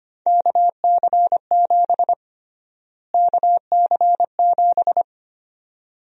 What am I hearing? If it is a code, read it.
Morse code, heard as a keyed tone, about 25 wpm, 700 Hz. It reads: KC7 KC7
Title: USA Call Sign Prefixes - Repeated 2x - 25wpm